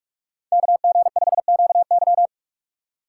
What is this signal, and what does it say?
Morse code, heard as a keyed tone, about 45 wpm, 700 Hz. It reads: KG5QY
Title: USA Call Signs - 5 Characters - 2x Character Spacing - 45wpm